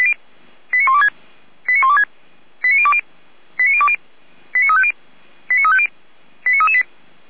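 Five-tone selective call (ZVEI), heard as quick runs of five tones, tones of about 70 ms. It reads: 89126 89126 89020 89020 89350 89350 89338
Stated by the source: Five-tone ZVEI selective call received on an SDR, fire brigade Feuerwehr Orlamünde.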